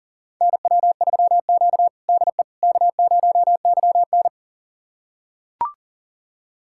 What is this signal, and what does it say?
Morse code, heard as a keyed tone, about 40 wpm, 700 Hz. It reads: NW3Q DE K0YN
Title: USA Callsign DE Callsign - 4 Characters - 40wpm